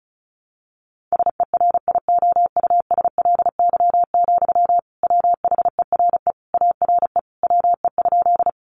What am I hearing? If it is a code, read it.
Morse code, heard as a keyed tone, about 34 wpm, 700 Hz. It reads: SERIOUSLY, WHERE ARE WE?
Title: Sentences from Top 1200 Words 35wpm